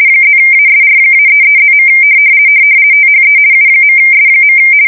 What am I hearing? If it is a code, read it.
RTTY, about 50 baud. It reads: +ASR-33+. THE U.S.
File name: fsk170-50B / teletype-corp